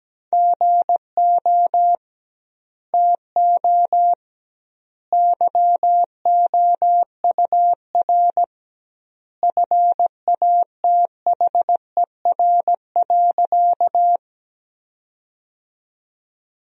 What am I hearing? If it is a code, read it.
Morse code, heard as a keyed tone, about 17 wpm, 700 Hz. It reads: GO TO YOUR FATHER.
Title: 2x Word-Spacing - Sentences from Top 300 Words - 17wpm